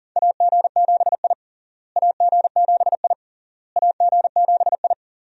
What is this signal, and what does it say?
Morse code, heard as a keyed tone, about 37 wpm, 700 Hz. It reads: AG7I AG7I AG7I
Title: USA Call Signs - 4 Characters - Repeated 3x - 40wpm